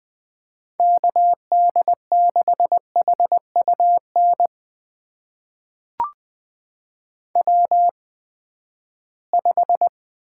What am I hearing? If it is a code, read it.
Morse code, heard as a keyed tone, about 20 wpm, spaced as for 14 wpm, 700 Hz. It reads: KD6HUN W5
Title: USA Call Signs - 8x Character Spacing - 20wpm